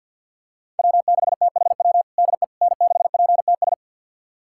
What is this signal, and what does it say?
Morse code, heard as a keyed tone, about 51 wpm, 700 Hz. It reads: W6THW DE N6PTS
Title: USA Callsign DE Callsign - 5 Characters - 50wpm